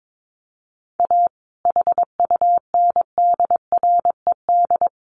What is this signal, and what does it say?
Morse code, heard as a keyed tone, about 22 wpm, 700 Hz. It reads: A HUNDRED
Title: A Sound Of Thunder 22wpm